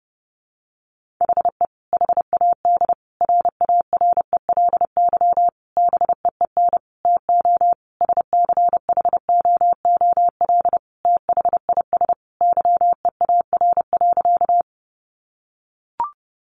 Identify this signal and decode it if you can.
Morse code, heard as a keyed tone, about 30 wpm, 700 Hz. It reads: HE HAD RARELY BEEN TO SCHOOL THIS YEAR.
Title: Sentences from Top 1400 Words 30wpm